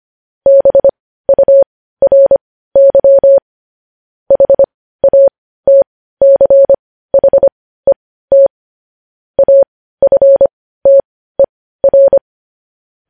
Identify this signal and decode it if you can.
Morse code, heard as a keyed tone, about 25 wpm, spaced as for 15 wpm, 550 Hz. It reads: BURY HATCHET AFTER
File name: podcast / News headlines-1x-25x15